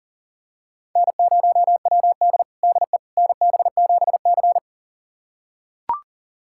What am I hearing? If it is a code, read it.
Morse code, heard as a keyed tone, about 40 wpm, 700 Hz. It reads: N0WD DE NB7C
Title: USA Callsign DE Callsign - 4 Characters - 40wpm